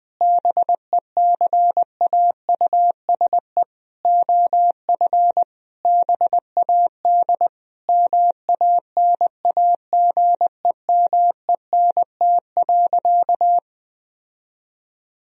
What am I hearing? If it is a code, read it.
Morse code, heard as a keyed tone, about 20 wpm, 700 Hz. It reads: BECAUSE OF BAD MANAGEMENT.
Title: Sentences from Top 1000 Words - Review Part 4 - 20wpm